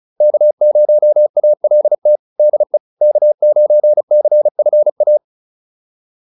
Morse code, heard as a keyed tone, about 37 wpm, 600 Hz. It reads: K0ALT DE K9CFA